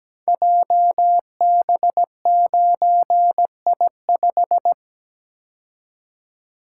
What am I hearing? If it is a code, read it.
Morse code, heard as a keyed tone, about 17 wpm, 700 Hz. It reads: JB9I5
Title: Code Groups - 5 Letter:Numbers - 17wpm